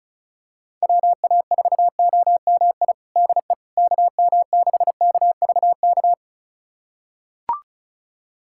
Morse code, heard as a keyed tone, about 36 wpm, 690 Hz. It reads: WA4OMI DE KM6KVK
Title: USA Callsign DE Callsign - 6 Characters - 35wpm